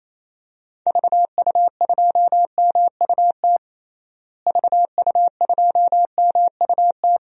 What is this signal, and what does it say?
Morse code, heard as a keyed tone, about 27 wpm, 700 Hz. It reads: VU2MUT VU2MUT
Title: DXCC-100 Call Signs - 6 Characters - Repeated 2x - 28wpm